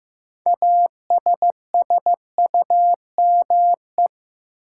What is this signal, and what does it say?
Morse code, heard as a keyed tone, about 15 wpm, 700 Hz. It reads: ASSUME
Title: Sentences from Top 1500 Words - New Word Drill Alt - 15wpm